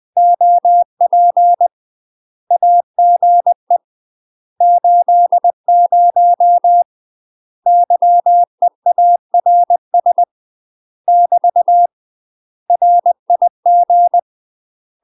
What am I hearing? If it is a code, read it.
Morse code, heard as a keyed tone, about 20 wpm, 700 Hz. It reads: OP AGE 80 YEARS <BT> RIG
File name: UK MASTER QSOs TRAINING / 20WPM / UK MASTER QSOs 0031